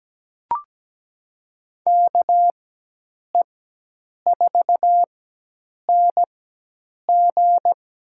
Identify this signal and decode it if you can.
Morse code, heard as a keyed tone, about 17 wpm, spaced as for 8 wpm, 700 Hz. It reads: KE4NG
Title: USA Call Signs - 4x Character Spacing - 17wpm